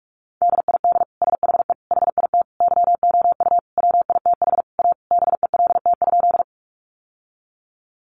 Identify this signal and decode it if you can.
Morse code, heard as a keyed tone, about 44 wpm, 700 Hz. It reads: DID SHE HIT YOU WITH A BELT?
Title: Sentences from Top 1400 Words 45wpm